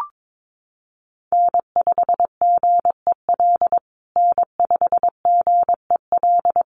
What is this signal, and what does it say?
Morse code, heard as a keyed tone, about 22 wpm, 700 Hz. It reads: N5GEL N5GEL
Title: USA Call Signs Repeated 2x 22wpm